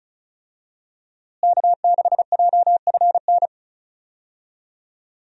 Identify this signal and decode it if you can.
Morse code, heard as a keyed tone, about 34 wpm, 700 Hz. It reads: K6JFN
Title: USA Call Signs - 5 Characters - Repeated 2x - 35wpm